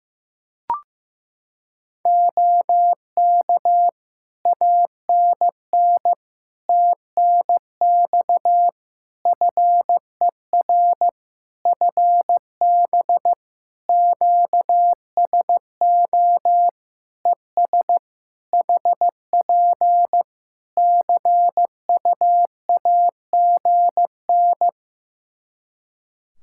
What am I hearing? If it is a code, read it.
Morse code, heard as a keyed tone, about 15 wpm, 700 Hz. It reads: OK ANN TNX FER FB QSO ES HP CUAGN
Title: K1USN SST QSO Practice 15wpm